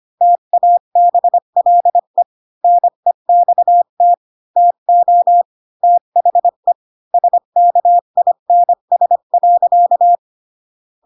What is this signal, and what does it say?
Morse code, heard as a keyed tone, about 25 wpm, 700 Hz. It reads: TABLE NEXT TO THE SKINS.
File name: Pinocchio / Pinocchio ch7 20 25 ik1hge